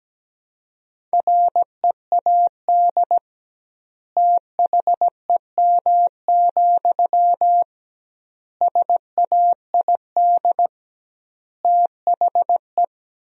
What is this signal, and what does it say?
Morse code, heard as a keyed tone, about 17 wpm, 700 Hz. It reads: READ THEM, SAID THE
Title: 2x Word-Spacing - Sentences from Top 400 Words - 17wpm